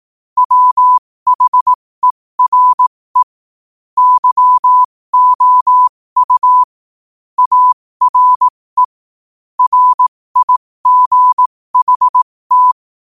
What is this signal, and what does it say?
Morse code, heard as a keyed tone, about 18 wpm, spaced as for 15 wpm, 1000 Hz. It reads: WHERE YOU ARE RIGHT
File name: qotdfiles / 15wpm